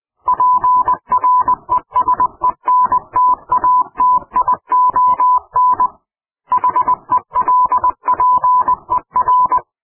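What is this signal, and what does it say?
Morse code, heard as a keyed tone, about 20 wpm, 1000 Hz. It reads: PRESENTATION HELPER
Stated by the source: Morse code clip offered as an mp3 ring tone, taken from PowerPoint sound clips.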